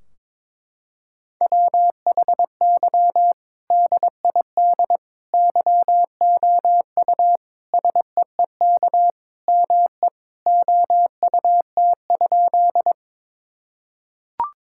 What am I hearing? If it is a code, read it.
Morse code, heard as a keyed tone, about 22 wpm, 700 Hz. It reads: WHY DID YOU SEEK ME OUT?
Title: Sentences from Top 1600 Words 22wpm